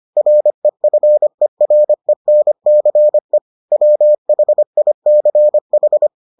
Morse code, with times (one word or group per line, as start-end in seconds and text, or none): 0.2-3.4 REFERENCE
3.7-6.1 WHICH